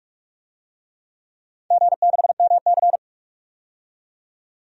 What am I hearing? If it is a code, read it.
Morse code, heard as a keyed tone, about 44 wpm, 700 Hz. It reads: G6MC
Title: DXCC-100 Call Signs - 4 Characters - 45wpm